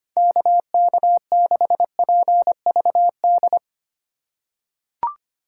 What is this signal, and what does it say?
Morse code, heard as a keyed tone, about 25 wpm, 700 Hz. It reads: KK6PVD
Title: USA Call Signs - 6 Characters - 2x Character Spacing - 25wpm